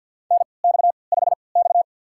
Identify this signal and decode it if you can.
Morse code, heard as a keyed tone, about 49 wpm, spaced as for 28 wpm, 700 Hz. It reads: NX5X
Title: USA Call Signs - 4 Characters - 3x Character Spacing - 50wpm